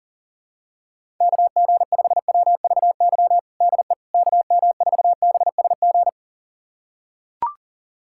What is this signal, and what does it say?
Morse code, heard as a keyed tone, about 40 wpm, 700 Hz. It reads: KG5WVY DE KM4BSG
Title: USA Callsign DE Callsign - 6 Characters - 40wpm